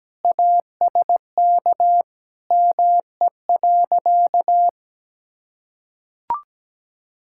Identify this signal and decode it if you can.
Morse code, heard as a keyed tone, about 17 wpm, 700 Hz. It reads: ASK ME.
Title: Sentences from Top 1000 Words - Review Part 5 - 17wpm